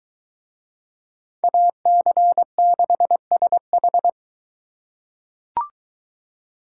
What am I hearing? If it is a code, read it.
Morse code, heard as a keyed tone, about 21 wpm, 700 Hz. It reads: AC6SH